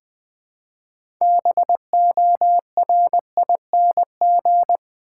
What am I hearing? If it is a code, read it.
Morse code, heard as a keyed tone, about 20 wpm, 700 Hz. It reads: BORING
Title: 6 Letter Words - First 3 Letters Given - 20wpm